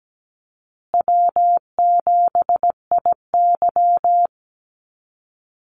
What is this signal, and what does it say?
Morse code, heard as a keyed tone, about 17 wpm, 700 Hz. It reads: W7IY
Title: Call Signs Repeated 2x - 17wpm